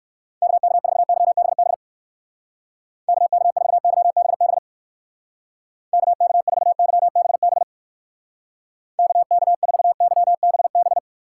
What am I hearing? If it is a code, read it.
Morse code, heard as a keyed tone, about 56 wpm, 700 Hz. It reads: KK4YBB KK4YBB KK4YBB KK4YBB
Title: Speed Racing - USA 2x3 Call Signs 45wpm